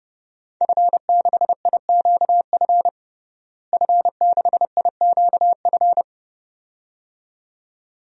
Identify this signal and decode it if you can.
Morse code, heard as a keyed tone, about 30 wpm, 700 Hz. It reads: F6IQF F6IQF
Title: DXCC-100 Call Signs - 5 Characters - Repeated 2x - 30wpm